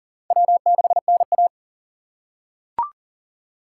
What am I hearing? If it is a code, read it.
Morse code, heard as a keyed tone, about 40 wpm, 700 Hz. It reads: W6NA
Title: USA Call Signs - 4 Characters - 3x Character Spacing - 40wpm